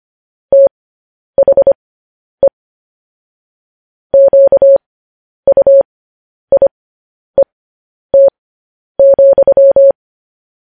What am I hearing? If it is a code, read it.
Morse code, heard as a keyed tone, about 25 wpm, spaced as for 10 wpm, 550 Hz. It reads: THE QUIET,